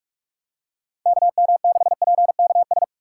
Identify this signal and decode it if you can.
Morse code, heard as a keyed tone, about 41 wpm, 700 Hz. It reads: KM6PKS